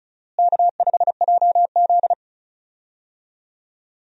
Morse code, heard as a keyed tone, about 36 wpm, 700 Hz. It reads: K5JZ
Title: USA Call Signs - 4 Characters - Repeated 2x - 35wpm